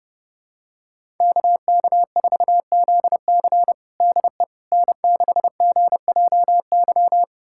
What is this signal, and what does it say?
Morse code, heard as a keyed tone, about 32 wpm, 700 Hz. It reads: KK4ZC DE N6GJY
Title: USA Callsign DE Callsign - 5 Characters - 30wpm